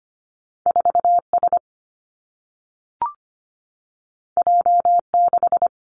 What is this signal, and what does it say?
Morse code, heard as a keyed tone, about 25 wpm, 700 Hz. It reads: 4 S J 6
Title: Two Challenging Letter-Number Combinations 25wpm